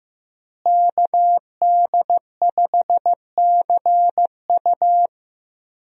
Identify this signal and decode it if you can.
Morse code, heard as a keyed tone, about 15 wpm, 700 Hz. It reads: KD5CU
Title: USA Call Signs - 5 Characters - Repeated 3x - 15wpm